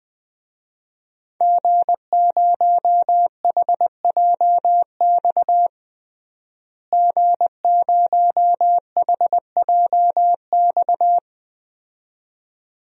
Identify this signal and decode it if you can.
Morse code, heard as a keyed tone, about 20 wpm, 700 Hz. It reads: G0HJX G0HJX
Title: DXCC-100 Call Signs - 5 Characters - Repeated 2x - 20wpm